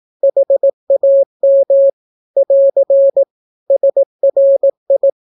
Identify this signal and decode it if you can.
Morse code, heard as a keyed tone, about 18 wpm, 550 Hz. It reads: HAM <AR> SRI
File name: naukapodstaw / skroty / skroty